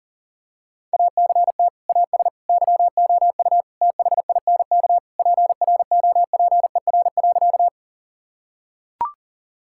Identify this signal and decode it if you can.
Morse code, heard as a keyed tone, about 40 wpm, 700 Hz. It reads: ACT AS YOU THINK PROPER.
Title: Sentences from Top 1000 Words - Review Part 5 - 40wpm